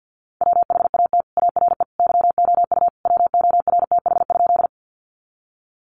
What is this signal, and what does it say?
Morse code, heard as a keyed tone, about 50 wpm, 700 Hz. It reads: WHAT ARE YOU WORTH?
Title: Sentences from Top 800 Words 50wpm